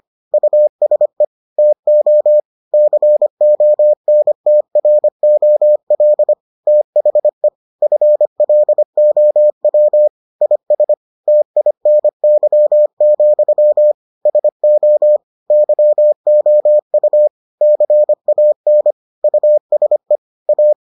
Morse code, heard as a keyed tone, about 25 wpm, 600 Hz. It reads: USE TO CONTROL THE FLOW IS TINY, SO YOU CAN USE A